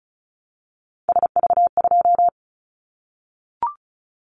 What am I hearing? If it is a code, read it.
Morse code, heard as a keyed tone, about 36 wpm, 700 Hz. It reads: SV2
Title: Callsign Prefixes for Top 50 DXCC - 35wpm